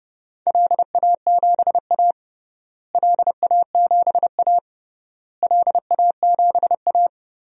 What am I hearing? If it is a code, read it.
Morse code, heard as a keyed tone, about 32 wpm, 700 Hz. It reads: LA7A LA7A LA7A
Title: DXCC-100 Call Signs - 4 Characters - Repeated 3x - 30wpm